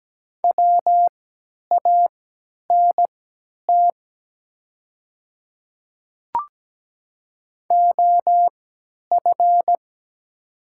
Morse code, heard as a keyed tone, about 17 wpm, spaced as for 10 wpm, 700 Hz. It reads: WANT OF
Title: 3x Character Spacing - Top 300 Words - 17wpm